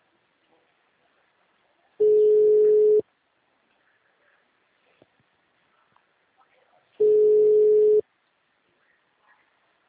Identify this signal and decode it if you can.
Ringback tone (Europe, 425 Hz).